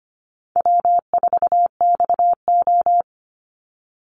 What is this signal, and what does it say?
Morse code, heard as a keyed tone, about 25 wpm, 700 Hz. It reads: W4XO
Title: Easy Call Signs Repeated 2x - 25wpm